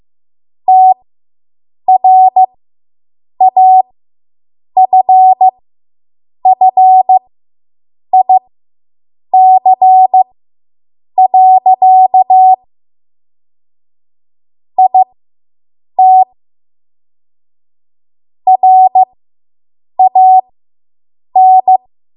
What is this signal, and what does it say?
Morse code, heard as a keyed tone, about 15 wpm, spaced as for 7 wpm, 750 Hz. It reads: TRAFFIC. IT RAN